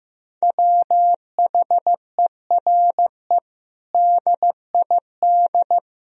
Morse code, heard as a keyed tone, about 15 wpm, 700 Hz. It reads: WHERE DID
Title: Sentences from Top 400 Words and Abbreviations 15wpm